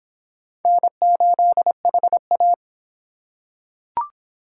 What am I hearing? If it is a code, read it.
Morse code, heard as a keyed tone, about 21 wpm, 700 Hz. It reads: N8HA
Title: Speed Racing - USA 1x2 Call Signs 17wpm